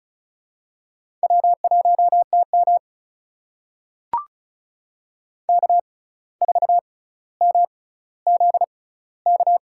Morse code, heard as a keyed tone, about 36 wpm, spaced as for 17 wpm, 700 Hz. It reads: W1TM K4MZK
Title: USA Call Signs - 6x Character Spacing - 35wpm